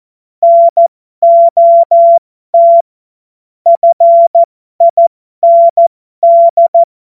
Morse code, heard as a keyed tone, about 14 wpm, spaced as for 12 wpm, 680 Hz. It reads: NOT FIND
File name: Morsekurs / Witze